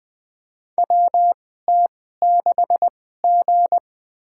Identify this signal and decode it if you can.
Morse code, heard as a keyed tone, about 20 wpm, spaced as for 14 wpm, 700 Hz. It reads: WT6G
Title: USA Call Signs - 2x Character Spacing - 20wpm